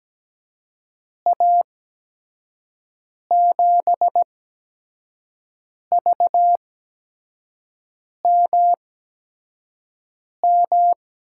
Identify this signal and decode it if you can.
Morse code, heard as a keyed tone, about 17 wpm, spaced as for 5 wpm, 700 Hz. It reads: A7VMM
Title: USA Call Signs - 6 Characters - 8x Character Spacing - 17wpm